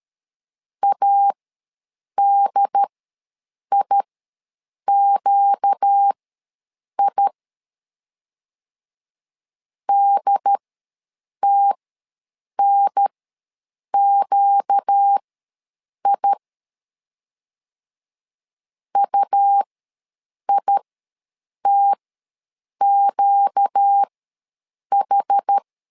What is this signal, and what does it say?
Morse code, heard as a keyed tone, about 13 wpm, spaced as for 7 wpm, 800 Hz. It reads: ADIQI DTNQI UITQH
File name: LEZIONI / 4 Q ws 12wpm 1912